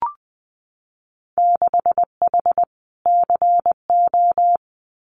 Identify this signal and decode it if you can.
Morse code, heard as a keyed tone, about 20 wpm, 700 Hz. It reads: 6H CO
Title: Winter Field Day Exchanges Encore 20wpm